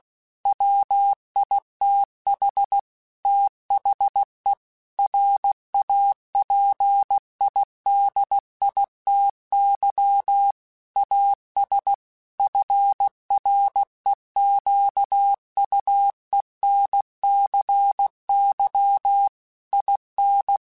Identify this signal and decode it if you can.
Morse code, heard as a keyed tone, about 16 wpm, 800 Hz. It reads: WITH THE RAPIDITY AS FREQUENCY IN
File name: Technical Subjects / Technical Subjects-mp3s / XL @15